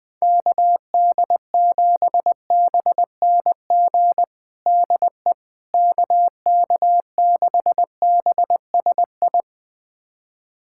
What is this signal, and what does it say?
Morse code, heard as a keyed tone, about 21 wpm, 700 Hz. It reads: KD7BNG DE KK6BSI
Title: USA Callsign DE Callsign - 6 Characters - 20wpm